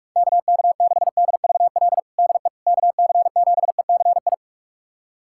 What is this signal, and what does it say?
Morse code, heard as a keyed tone, about 44 wpm, 700 Hz. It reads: KK6DVL DE KC7ECI
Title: USA Callsign DE Callsign - 6 Characters - 45wpm